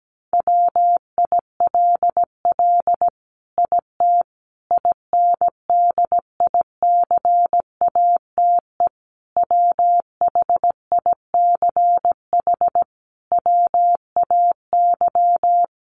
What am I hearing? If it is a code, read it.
Morse code, heard as a keyed tone, about 17 wpm, 700 Hz. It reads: WILL IT INDICATE WHICH WAY
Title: Sentences from Top 1300 Words 17wpm